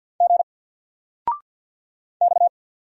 Morse code, heard as a keyed tone, about 49 wpm, 700 Hz. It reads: G X
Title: Single Character (Letter:Number:Punctuation:Prosign) 50wpm